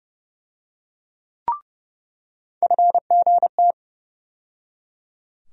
Morse code, heard as a keyed tone, about 31 wpm, 700 Hz. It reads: FGT
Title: USA Call Sign Postfixes - 3 Letters 30wpm